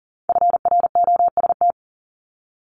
Morse code, heard as a keyed tone, about 40 wpm, 700 Hz. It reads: FROST
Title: Multi-letter Phonemes Repeated 3x 40wpm